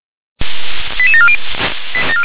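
Five-tone selective call (ZVEI), heard as quick runs of five tones, tones of about 70 ms. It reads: 88740
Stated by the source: Five-tone ZVEI selective call 88914, received by SDR.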